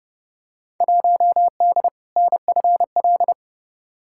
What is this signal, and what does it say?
Morse code, heard as a keyed tone, about 30 wpm, 700 Hz. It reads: 1D NFL
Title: Field Day Exchanges - 30wpm